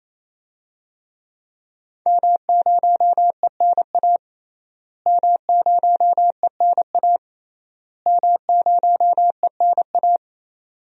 Morse code, heard as a keyed tone, about 28 wpm, 700 Hz. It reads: M0ENA M0ENA M0ENA
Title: DXCC-100 Call Signs - 5 Characters - Repeated 3x - 28wpm